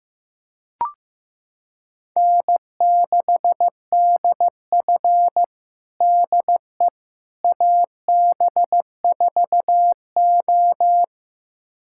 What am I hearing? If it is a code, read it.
Morse code, heard as a keyed tone, about 15 wpm, 700 Hz. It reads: N6DF DE AB4O
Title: USA Callsign DE Callsign - 4 Characters - 15wpm